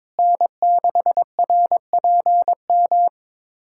Morse code, heard as a keyed tone, about 22 wpm, 700 Hz. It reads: N6RPM
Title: USA Callsign DE Callsign - 5 Characters - 22wpm